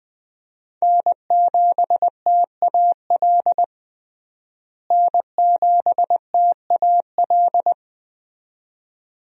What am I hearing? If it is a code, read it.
Morse code, heard as a keyed tone, about 20 wpm, 700 Hz. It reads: N7TAL N7TAL
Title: USA Call Signs - 5 Characters - Repeated 2x - 20wpm